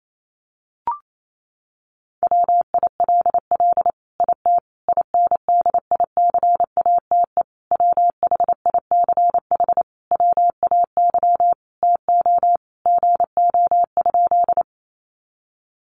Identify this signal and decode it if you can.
Morse code, heard as a keyed tone, about 28 wpm, 700 Hz. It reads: WILL IT INDICATE WHICH WAY TO GO?
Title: Sentences from Top 1300 Words 28wpm